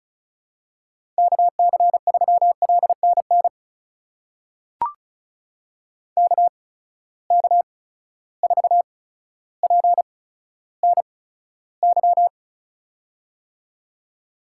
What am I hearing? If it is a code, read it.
Morse code, heard as a keyed tone, about 36 wpm, spaced as for 15 wpm, 700 Hz. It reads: KC3LNN KK4PNY
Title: USA Call Signs - 6 Characters - 8x Character Spacing - 35wpm